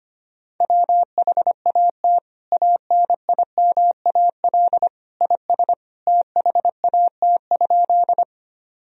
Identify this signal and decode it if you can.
Morse code, heard as a keyed tone, about 25 wpm, 700 Hz. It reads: WHAT ANIMAL IS THAT?
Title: Sentences from Top 1000 Words - Review Part 1 - 25wpm